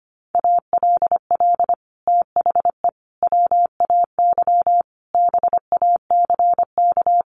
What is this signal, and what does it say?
Morse code, heard as a keyed tone, about 25 wpm, 700 Hz. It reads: ALL THE WAY BACK
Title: Sets of 5 Words Encore 25wpm